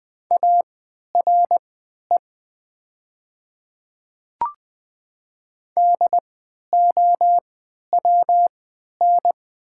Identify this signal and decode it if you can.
Morse code, heard as a keyed tone, about 20 wpm, spaced as for 11 wpm, 700 Hz. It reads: ARE DOWN
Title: 3x Character Spacing - Top 100 Words - 20wpm